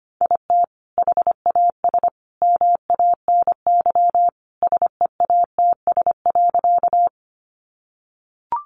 Morse code, heard as a keyed tone, about 25 wpm, 700 Hz. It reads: IT HAS MANY SEATS.
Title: Additional Easy Phrases 25wpm